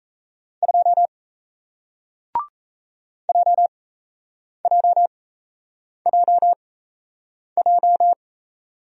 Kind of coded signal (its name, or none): Morse code